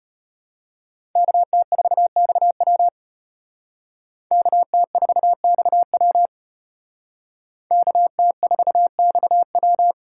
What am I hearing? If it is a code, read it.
Morse code, heard as a keyed tone, about 32 wpm, 700 Hz. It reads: KT4XW KT4XW KT4XW